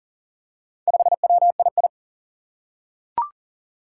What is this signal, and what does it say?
Morse code, heard as a keyed tone, about 38 wpm, 700 Hz. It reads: 5WII
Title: Four Challenging Letter-Number Combinations 40wpm